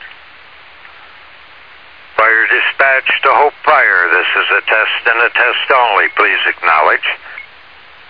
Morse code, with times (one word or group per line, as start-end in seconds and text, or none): none